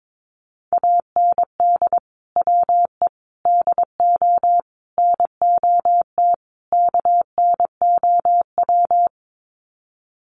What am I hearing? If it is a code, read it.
Morse code, heard as a keyed tone, about 22 wpm, 700 Hz. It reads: AND WE DO NOT KNOW
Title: Sets of 5 Words Encore 22wpm